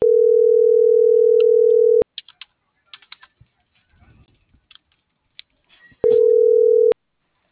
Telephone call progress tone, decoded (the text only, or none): ringback tone